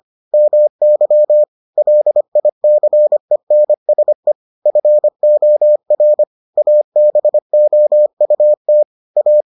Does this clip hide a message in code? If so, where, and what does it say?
Yes: Morse code from 0.3 to 9.4 s, heard as a keyed tone, about 25 wpm, 600 Hz. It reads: MY LICENSE FOR ABOUT A